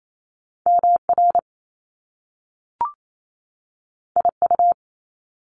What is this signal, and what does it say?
Morse code, heard as a keyed tone, about 28 wpm, 700 Hz. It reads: MR IU
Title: Top Two Letter Combinations in Words 28wpm